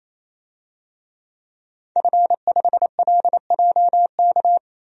Morse code, heard as a keyed tone, about 28 wpm, 700 Hz. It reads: F5LJK
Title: DXCC-100 Call Signs - 5 Characters - 28wpm